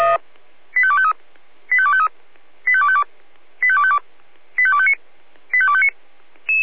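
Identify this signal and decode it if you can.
Five-tone selective call (ZVEI), heard as quick runs of five tones, tones of about 70 ms. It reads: 85253 85253 85252 85252 85259 85259
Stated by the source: ZVEI five-tone selective call received on SDR.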